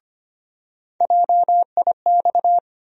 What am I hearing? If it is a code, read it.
Morse code, heard as a keyed tone, about 25 wpm, 700 Hz. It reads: JIX
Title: USA Call Sign Postfixes - 3 Letters 25wpm